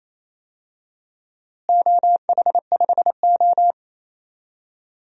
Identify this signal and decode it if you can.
Morse code, heard as a keyed tone, about 28 wpm, 700 Hz. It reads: OH5O
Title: DXCC-100 Call Signs - 4 Characters - Repeated 2x - 28wpm